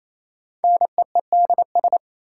Morse code, heard as a keyed tone, about 28 wpm, 700 Hz. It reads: NEEDS
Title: Sets of 3 Words - New Word Drill - 28wpm